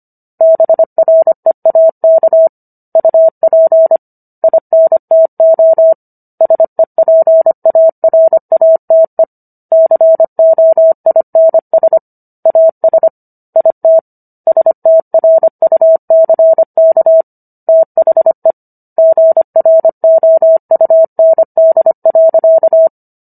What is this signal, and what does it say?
Morse code, heard as a keyed tone, about 25 wpm, 650 Hz. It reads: BREAK UP INTO SEPARATE COINS AS IT STRUCK THE GROUND.